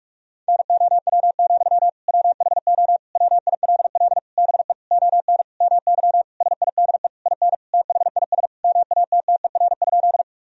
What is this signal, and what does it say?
Morse code, heard as a keyed tone, about 46 wpm, 700 Hz. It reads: NOW, WHO WILL BE ON MY SIDE IN THIS MATTER?
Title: Sentences from Top 1000 Words - Review Part 1 - 45wpm